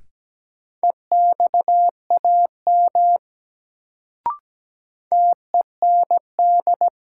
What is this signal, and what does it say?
Morse code, heard as a keyed tone, about 17 wpm, 700 Hz. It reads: EXAM TEND
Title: Sentences from Top 1000 Words - Review Part 5 - New Word Drill - 17wpm